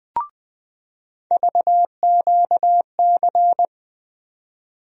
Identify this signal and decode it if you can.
Morse code, heard as a keyed tone, about 20 wpm, 700 Hz. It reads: VQC
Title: USA Call Sign Postfixes - 3 Letters 20wpm